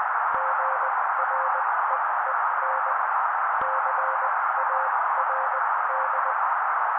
Morse code, heard as a keyed tone, about 20 wpm, spaced as for 16 wpm, 530 Hz. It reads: GREEN CARD
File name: daily / 16wpm noise